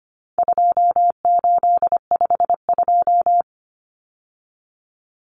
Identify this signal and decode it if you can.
Morse code, heard as a keyed tone, about 25 wpm, 700 Hz. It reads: 2852
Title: Common Cut Numbers - 4-Digit 25wpm